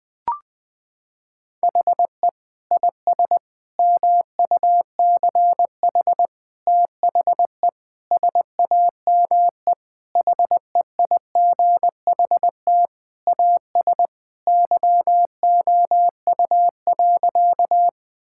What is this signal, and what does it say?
Morse code, heard as a keyed tone, about 20 wpm, 700 Hz. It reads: HE IS MUCH THE SAME HEIGHT AS YOU.
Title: Sentences from Top 1600 Words 20wpm